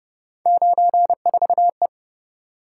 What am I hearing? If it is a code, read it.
Morse code, heard as a keyed tone, about 30 wpm, 700 Hz. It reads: 94E